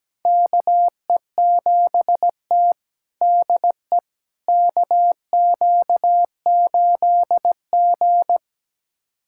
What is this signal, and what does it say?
Morse code, heard as a keyed tone, about 17 wpm, 700 Hz. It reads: KE7T DE KQ8G
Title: USA Callsign DE Callsign - 4 Characters - 17wpm